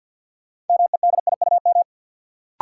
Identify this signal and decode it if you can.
Morse code, heard as a keyed tone, about 48 wpm, 700 Hz. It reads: MEDIUM
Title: Sentences from Top 900 Words - New Word Drill - 50wpm